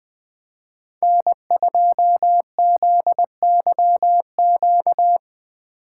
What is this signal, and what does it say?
Morse code, heard as a keyed tone, about 20 wpm, 700 Hz. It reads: N2ZYQ
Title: Speed Racing - USA 1x3 Call Signs 17wpm